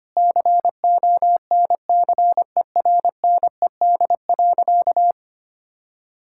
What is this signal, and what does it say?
Morse code, heard as a keyed tone, about 25 wpm, 700 Hz. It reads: CONCERNED.
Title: Sentences from Top 1000 Words - Review Part 6 - 25wpm